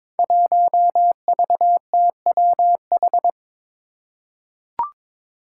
Morse code, heard as a keyed tone, about 22 wpm, 700 Hz. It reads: 1VTWH
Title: Five Challenging Letter-Number Combinations 22wpm